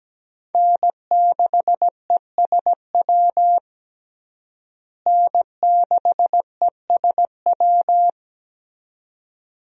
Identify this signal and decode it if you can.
Morse code, heard as a keyed tone, about 17 wpm, 700 Hz. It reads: N6ESW N6ESW
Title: USA Call Signs - 5 Characters - Repeated 2x - 17wpm